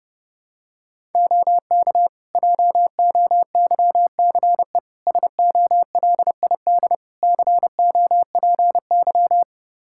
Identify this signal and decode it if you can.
Morse code, heard as a keyed tone, about 30 wpm, 700 Hz. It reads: OK JOYCE SOLID COPY
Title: Practice QSO - Exchange 3 - More Info - 30wpm